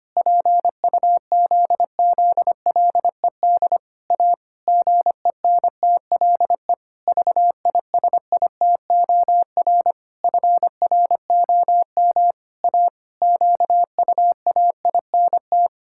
Morse code, heard as a keyed tone, about 25 wpm, 700 Hz. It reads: PUZZLED A GENTLE VISITOR FROM A QUAINT
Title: Pangram Challenge v2 - 25wpm